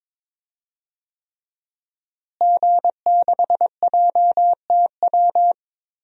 Morse code, heard as a keyed tone, about 21 wpm, 700 Hz. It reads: G6JTW
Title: DXCC-100 Call Signs - 5 Characters - Repeated 3x - 22wpm